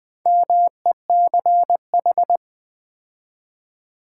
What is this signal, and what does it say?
Morse code, heard as a keyed tone, about 20 wpm, 700 Hz. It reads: MECH
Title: Build A Word - 2 Letters At A Time - 8-Letter Words - 20wpm